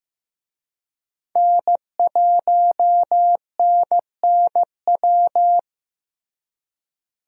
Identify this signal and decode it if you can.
Morse code, heard as a keyed tone, about 15 wpm, 700 Hz. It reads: N1NNW
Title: USA Call Signs - 5 Characters - Repeated 3x - 15wpm